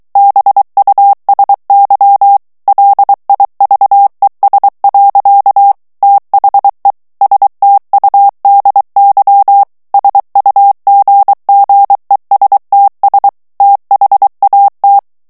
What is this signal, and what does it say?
Morse code, heard as a keyed tone, about 23 wpm, 800 Hz. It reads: BUSY LIVES. THE STUDY SUGGESTS THAT
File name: morse mp3 / ANGLAIS texte clair 1 vitesse 22 mots minute